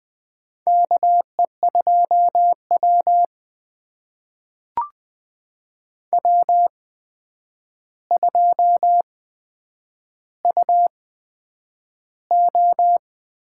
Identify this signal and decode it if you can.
Morse code, heard as a keyed tone, about 20 wpm, spaced as for 9 wpm, 700 Hz. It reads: KE2W W2UO
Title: USA Call Signs - 4 Characters - 8x Character Spacing - 20wpm